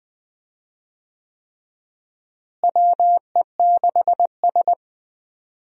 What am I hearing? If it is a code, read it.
Morse code, heard as a keyed tone, about 20 wpm, 700 Hz. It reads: WE6S
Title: USA Call Signs - 4 Characters - Repeated 3x - 20wpm